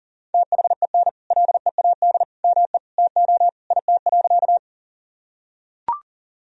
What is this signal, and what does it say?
Morse code, heard as a keyed tone, about 37 wpm, 700 Hz. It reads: THEN LEAD ME TO IT.
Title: Sentences from Top 1000 Words - Review Part 2 - 40wpm